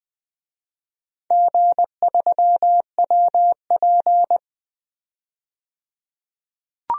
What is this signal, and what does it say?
Morse code, heard as a keyed tone, about 20 wpm, 700 Hz. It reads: G3WP
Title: DXCC-100 Call Signs - 4 Characters - 20wpm